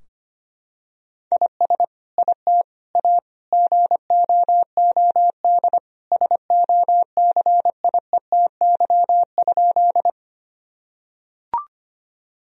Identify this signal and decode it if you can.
Morse code, heard as a keyed tone, about 25 wpm, 700 Hz. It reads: IS IT A GOOD SOCIETY?
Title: Sentences from Top 1000 Words - Review Part 4 - 25wpm